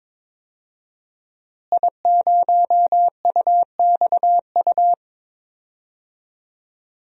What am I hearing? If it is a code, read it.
Morse code, heard as a keyed tone, about 22 wpm, 700 Hz. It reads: I0UXU
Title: DXCC-100 Call Signs - 5 Characters - Repeated 2x - 22wpm